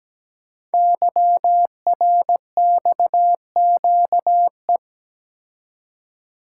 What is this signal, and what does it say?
Morse code, heard as a keyed tone, about 17 wpm, 700 Hz. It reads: YRXQE
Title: Code Groups - 5 Letters - 17wpm